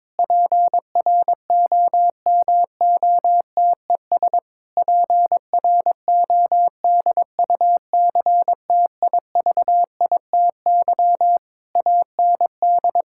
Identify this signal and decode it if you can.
Morse code, heard as a keyed tone, about 22 wpm, 700 Hz. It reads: PROMOTES PRODUCTIVITY AND
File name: Difficult Sentences with Hints - 22wpm